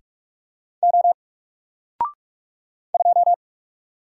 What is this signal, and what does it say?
Morse code, heard as a keyed tone, about 46 wpm, 700 Hz. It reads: O 2
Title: Single Letter-Number 45wpm